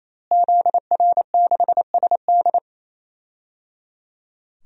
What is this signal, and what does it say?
Morse code, heard as a keyed tone, about 28 wpm, 700 Hz. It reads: ZR6SD
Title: DXCC-100 Call Signs - 5 Characters - Repeated 3x - 28wpm